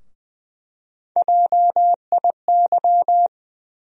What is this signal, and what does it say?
Morse code, heard as a keyed tone, about 21 wpm, 700 Hz. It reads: JIY
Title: USA Call Sign Postfixes - 3 Letters 20wpm